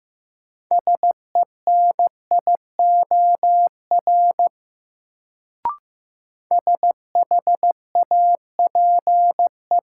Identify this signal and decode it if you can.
Morse code, heard as a keyed tone, about 15 wpm, 700 Hz. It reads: SENIOR SHAPE
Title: Sets of 5 Words Encore - New Word Drill - 15wpm